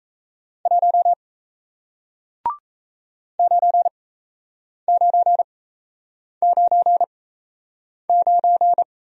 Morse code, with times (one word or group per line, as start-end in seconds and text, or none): 0.7-1.1 1
3.4-8.8 9999